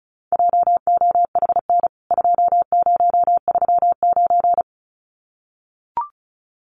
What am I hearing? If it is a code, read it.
Morse code, heard as a keyed tone, about 34 wpm, 700 Hz. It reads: JOHN 2039
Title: CWT Exchange 35wpm